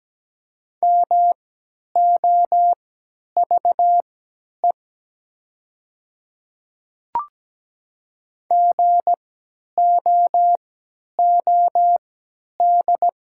Morse code, heard as a keyed tone, about 17 wpm, spaced as for 10 wpm, 700 Hz. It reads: MOVE GOOD
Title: 3x Character Spacing - Top 200 Words - 17wpm